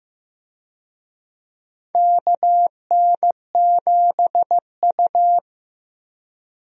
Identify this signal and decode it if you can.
Morse code, heard as a keyed tone, about 15 wpm, 700 Hz. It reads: KN7U
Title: USA Call Signs - 4 Characters - Repeated 3x - 15wpm